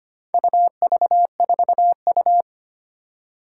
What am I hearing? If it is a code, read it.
Morse code, heard as a keyed tone, about 25 wpm, 700 Hz. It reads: UV4U
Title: Four Challenging Letter-Number Combinations 25wpm